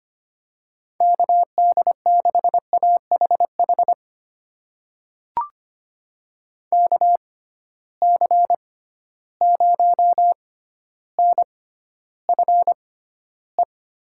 Morse code, heard as a keyed tone, about 25 wpm, spaced as for 13 wpm, 700 Hz. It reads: KD6AHH KC0NFE
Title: USA Call Signs - 6x Character Spacing - 25wpm